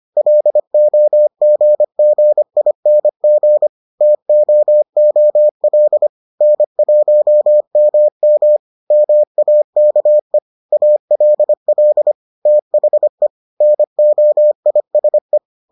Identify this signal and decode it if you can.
Morse code, heard as a keyed tone, about 25 wpm, 600 Hz. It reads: LOGGING TOOL N1MM MAKE ALL THE NOISE